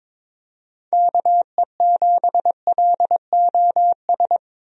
Morse code, heard as a keyed tone, about 22 wpm, 700 Hz. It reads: KE7LOS
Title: USA Call Signs - 6 Characters - Repeated 3x - 22wpm